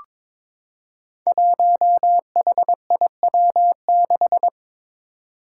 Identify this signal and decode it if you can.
Morse code, heard as a keyed tone, about 21 wpm, 700 Hz. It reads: 1HIW6
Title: Five Challenging Letter-Number Combinations 22wpm